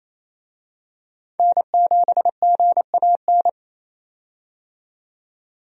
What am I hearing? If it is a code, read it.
Morse code, heard as a keyed tone, about 28 wpm, 700 Hz. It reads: N7GAN
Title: USA Call Signs - 5 Characters - 28wpm